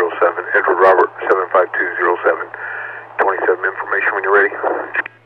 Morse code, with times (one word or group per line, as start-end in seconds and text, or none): none